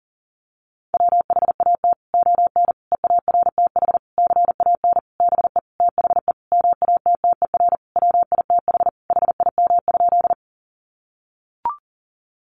Morse code, heard as a keyed tone, about 40 wpm, 700 Hz. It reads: WHAT ON EARTH CAN BE THE MATTER WITH HIM?
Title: Sentences from Top 700 Words 40wpm